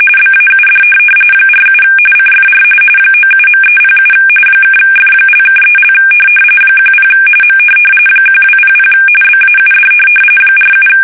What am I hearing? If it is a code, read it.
RTTY, about 45.45 baud. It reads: BAUD 47 CFR 97.307 (F)(3) 10 M 1200 BAUD 47
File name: fsk850 / teletype-RTTY